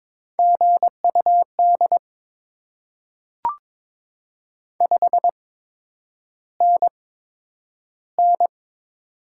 Morse code, heard as a keyed tone, about 22 wpm, spaced as for 9 wpm, 700 Hz. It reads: GUD 5NN
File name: Top 20 QSO Elements - 8x Character Spacing - 22wpm